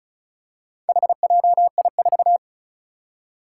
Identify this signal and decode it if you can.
Morse code, heard as a keyed tone, about 36 wpm, 700 Hz. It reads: HJI4
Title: Four Challenging Letter-Number Combinations 35wpm